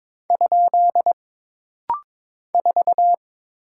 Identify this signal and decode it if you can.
Morse code, heard as a keyed tone, about 22 wpm, 700 Hz. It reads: ? 4
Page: This rapid-fire timing is geared to push your ICR (Instant Character Recognition) skills.